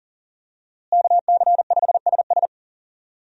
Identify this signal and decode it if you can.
Morse code, heard as a keyed tone, about 38 wpm, 700 Hz. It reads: KC5SS
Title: USA Call Signs - 5 Characters - Repeated 2x - 40wpm